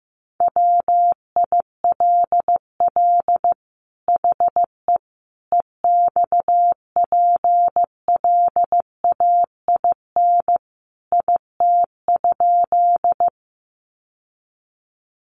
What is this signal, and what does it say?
Morse code, heard as a keyed tone, about 15 wpm, 700 Hz. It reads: WILL HE EXPLAIN IT?
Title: Sentences from Top 1100 Words 15wpm